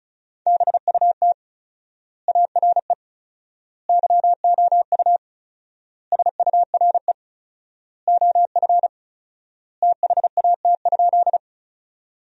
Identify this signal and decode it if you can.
Morse code, heard as a keyed tone, about 36 wpm, 700 Hz. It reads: BUT ARE YOU SURE OF THAT?
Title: 4x Word-Spacing - Sentences from Top 300 Words - 35wpm